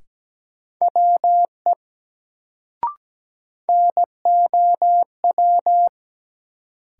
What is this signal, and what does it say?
Morse code, heard as a keyed tone, about 17 wpm, 700 Hz. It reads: WE NOW
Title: Sentences from Top 100 Words - New Word Drill - 17wpm